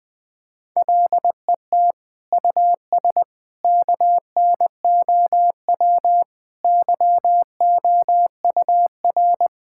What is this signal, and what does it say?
Morse code, heard as a keyed tone, about 20 wpm, 700 Hz. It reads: LET US KNOW YOUR
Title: Sentences with 7-Word Phrases - Spoken Beginning - 20wpm